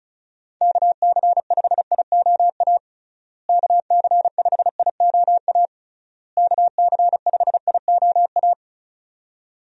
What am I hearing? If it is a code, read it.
Morse code, heard as a keyed tone, about 36 wpm, 700 Hz. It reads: KC5IOA KC5IOA KC5IOA
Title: USA Call Signs - 6 Characters - Repeated 3x - 35wpm